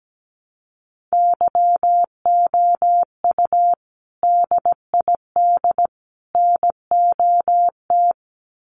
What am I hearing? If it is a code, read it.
Morse code, heard as a keyed tone, about 17 wpm, 700 Hz. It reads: YOU DID NOT
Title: Sets of 3 Words 17wpm